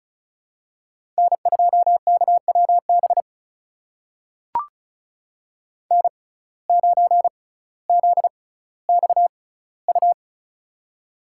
Morse code, heard as a keyed tone, about 36 wpm, spaced as for 18 wpm, 700 Hz. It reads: N2KWB N9ZXU
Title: USA Call Signs - 5 Characters - 6x Character Spacing - 35wpm